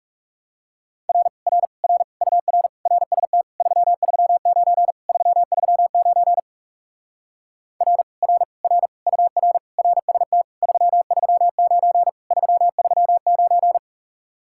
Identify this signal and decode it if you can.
Morse code, heard as a keyed tone, about 40 wpm, 700 Hz. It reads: R R R UR RST 339 339 R R R UR RST 339 339